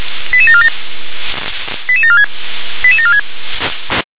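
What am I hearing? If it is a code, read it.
Five-tone selective call (ZVEI), heard as quick runs of five tones, tones of about 70 ms. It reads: 88747 88746 88746
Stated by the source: Five-tone ZVEI selective call received with an SDR.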